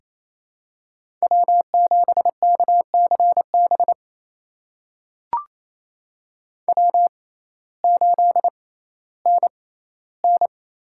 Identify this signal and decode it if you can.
Morse code, heard as a keyed tone, about 28 wpm, spaced as for 15 wpm, 700 Hz. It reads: W7KCB W8NN
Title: USA Call Signs - 6x Character Spacing - 28wpm